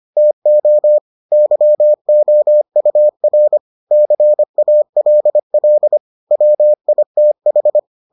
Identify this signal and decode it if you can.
Morse code, heard as a keyed tone, about 25 wpm, 600 Hz. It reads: TO YOUR CALL WITH